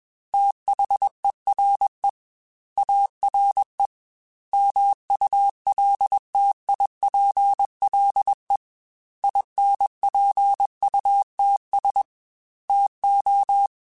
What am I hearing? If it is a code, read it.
Morse code, heard as a keyed tone, about 21 wpm, 800 Hz. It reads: THERE ARE MULTIPLE INPUTS TO
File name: Technical Subjects / Technical Subjects-mp3s / AND Gate @20